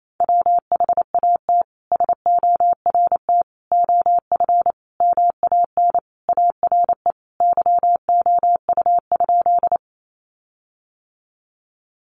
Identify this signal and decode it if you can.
Morse code, heard as a keyed tone, about 28 wpm, 700 Hz. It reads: WHAT SORT OF MAN ARE YOU?
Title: Sentences from Top 1300 Words 28wpm